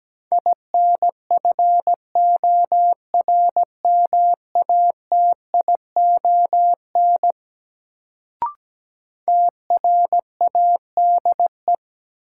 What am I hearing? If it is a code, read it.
Morse code, heard as a keyed tone, about 17 wpm, 700 Hz. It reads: INFORMATION TRADE
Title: Sentences from Top 600 Words - New Word Drill - 17wpm